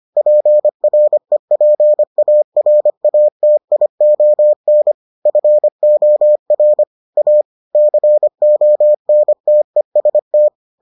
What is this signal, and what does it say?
Morse code, heard as a keyed tone, about 25 wpm, 600 Hz. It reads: PREPARATION FOR A CONTEST